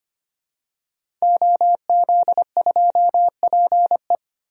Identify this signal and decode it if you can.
Morse code, heard as a keyed tone, about 25 wpm, 700 Hz. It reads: OZ2PE
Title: DXCC-100 Call Signs - 5 Characters - 25wpm